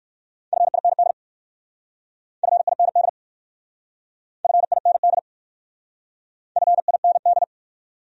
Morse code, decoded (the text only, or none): FIND FIND FIND FIND